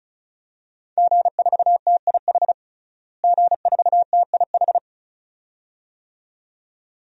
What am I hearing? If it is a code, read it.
Morse code, heard as a keyed tone, about 34 wpm, 700 Hz. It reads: G4TIH G4TIH
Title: DXCC-100 Call Signs - 5 Characters - Repeated 2x - 35wpm